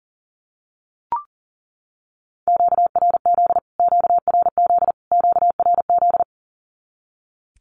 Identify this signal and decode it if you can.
Morse code, heard as a keyed tone, about 40 wpm, 700 Hz. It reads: QRZ QRZ QRZ
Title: Q-Codes Repeated 3x 40wpm